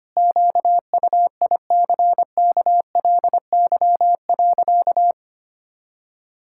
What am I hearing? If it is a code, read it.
Morse code, heard as a keyed tone, about 25 wpm, 700 Hz. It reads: QUICKLY.
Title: Pangram Challenge v2 - 25wpm